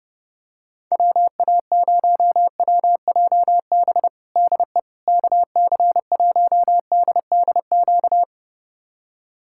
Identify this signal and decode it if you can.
Morse code, heard as a keyed tone, about 30 wpm, 700 Hz. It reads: WA0WJB DE KC1DDQ
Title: USA Callsign DE Callsign - 6 Characters - 30wpm